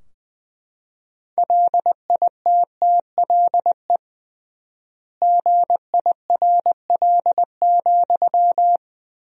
Morse code, heard as a keyed tone, about 20 wpm, 700 Hz. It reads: LITTLE GIRL,
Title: 3x Word-Spacing - Sentences from Top 400 Words - 20wpm